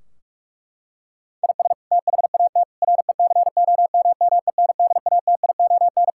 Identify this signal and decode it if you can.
Morse code, heard as a keyed tone, about 41 wpm, 700 Hz. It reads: IS THAT RECOMMENDATION
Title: Sentences from Top 1600 Words 45wpm